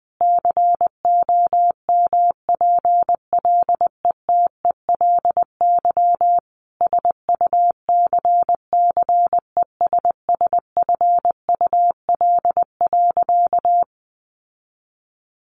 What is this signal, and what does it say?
Morse code, heard as a keyed tone, about 19 wpm, 700 Hz. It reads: COMPLETELY SUCCESSFUL.
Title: Sentences from Top 1100 Words 20wpm